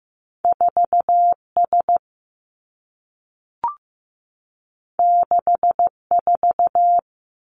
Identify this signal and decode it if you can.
Morse code, heard as a keyed tone, about 15 wpm, 700 Hz. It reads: 4 S 6 4
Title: Two Challenging Letter-Number Combinations 15wpm